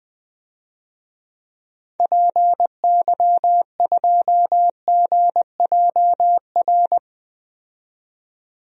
Morse code, heard as a keyed tone, about 21 wpm, 700 Hz. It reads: PY2GJR
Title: DXCC-100 Call Signs - 6 Characters - 20wpm